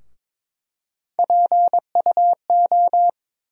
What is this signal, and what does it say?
Morse code, heard as a keyed tone, about 23 wpm, 700 Hz. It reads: PUO
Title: USA Call Sign Postfixes - 3 Letters 22wpm